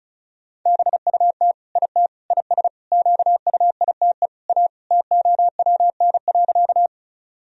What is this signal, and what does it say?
Morse code, heard as a keyed tone, about 36 wpm, 700 Hz. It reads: BUT IT IS QUITE A TOWN.
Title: Sentences from Top 1000 Words - Review Part 2 - 35wpm